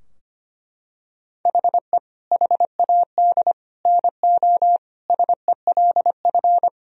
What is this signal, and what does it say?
Morse code, heard as a keyed tone, about 25 wpm, 700 Hz. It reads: HE HAD NO SELF
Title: Sentences from Top 1000 Words - Review Part 3 - 25wpm